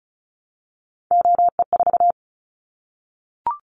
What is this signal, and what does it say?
Morse code, heard as a keyed tone, about 36 wpm, 700 Hz. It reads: OE4
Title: Callsign Prefixes for Top 50 DXCC - 35wpm